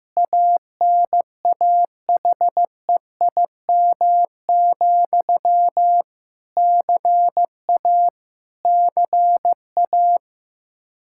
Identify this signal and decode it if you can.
Morse code, heard as a keyed tone, about 15 wpm, 700 Hz. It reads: ANAHEIM, CA CA
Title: Practice QSO - Exchange 1 - Beginning - 15wpm